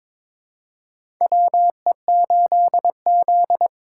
Morse code, heard as a keyed tone, about 22 wpm, 700 Hz. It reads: WE8Z
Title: USA Call Signs - 4 Characters - 4x Character Spacing - 22wpm